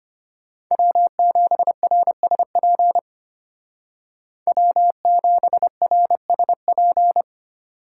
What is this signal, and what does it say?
Morse code, heard as a keyed tone, about 32 wpm, 700 Hz. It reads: W7RSP W7RSP